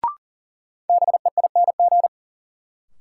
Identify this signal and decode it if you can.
Morse code, heard as a keyed tone, about 40 wpm, 700 Hz. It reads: BEING
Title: Sentences from Top 600 Words - New Word Drill - 40wpm